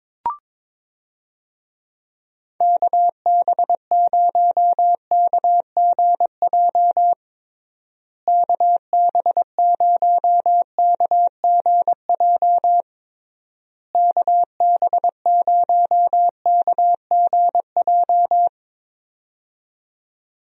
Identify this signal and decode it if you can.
Morse code, heard as a keyed tone, about 22 wpm, 700 Hz. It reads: KB0KGJ KB0KGJ KB0KGJ
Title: USA Call Signs - 6 Characters - Repeated 3x - 22wpm